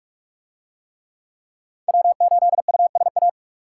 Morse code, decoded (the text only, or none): W8USA